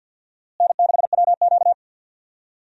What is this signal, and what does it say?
Morse code, heard as a keyed tone, about 50 wpm, 700 Hz. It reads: N6WQ
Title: USA Call Signs - 4 Characters - Repeated 3x - 50wpm